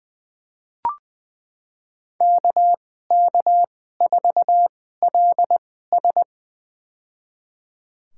Morse code, heard as a keyed tone, about 20 wpm, spaced as for 15 wpm, 700 Hz. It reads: KK4LS
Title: USA Call Signs - 5 Characters - 2x Character Spacing - 20wpm